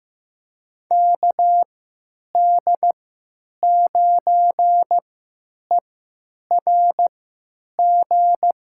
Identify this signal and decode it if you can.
Morse code, heard as a keyed tone, about 15 wpm, spaced as for 9 wpm, 700 Hz. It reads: KD9ERG
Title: USA Call Signs - 6 Characters - 3x Character Spacing - 15wpm